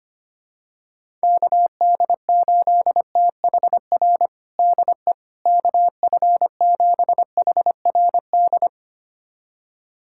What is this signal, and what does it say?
Morse code, heard as a keyed tone, about 25 wpm, 700 Hz. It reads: KD8THR DE KF7HRD
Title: USA Callsign DE Callsign - 6 Characters - 25wpm